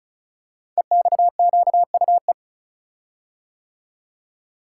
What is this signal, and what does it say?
Morse code, heard as a keyed tone, about 36 wpm, 700 Hz. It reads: EXQUE
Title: Code Groups - 5 Letter:Numbers - 35wpm